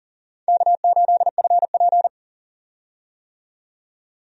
Morse code, heard as a keyed tone, about 39 wpm, 700 Hz. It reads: K8FP
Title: USA Call Signs - 4 Characters - 40wpm